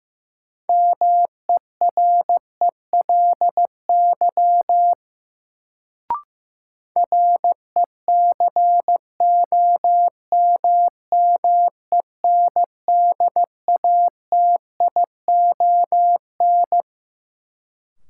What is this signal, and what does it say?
Morse code, heard as a keyed tone, about 15 wpm, 700 Hz. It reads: MERELY RECOMMENDATION
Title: Sentences from Top 1400 Words - New Word Drill - 15wpm